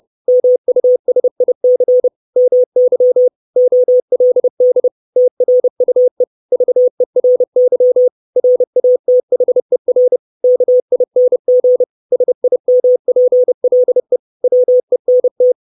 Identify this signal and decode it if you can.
Morse code, heard as a keyed tone, about 30 wpm, 500 Hz. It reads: MUSIC MY OLD TRUE VERY RATHER KING SIMPLE WENT
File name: 632 Words 30WPM